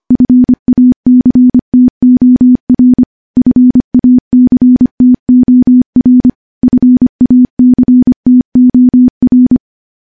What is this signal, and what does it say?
Morse code, heard as a keyed tone, about 25 wpm, 250 Hz. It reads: FACTOR FACTOR FACTOR
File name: morse / morse-training-25WPM